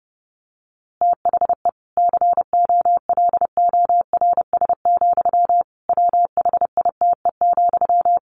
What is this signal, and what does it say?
Morse code, heard as a keyed tone, about 30 wpm, 700 Hz. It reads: THE COLORS, WHITE,
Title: A Sound Of Thunder 30wpm